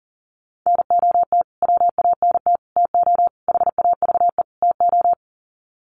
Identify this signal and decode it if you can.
Morse code, heard as a keyed tone, about 40 wpm, 700 Hz. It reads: NOT WANT TO HAVE TO
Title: Sets of 5 Words Encore2 40wpm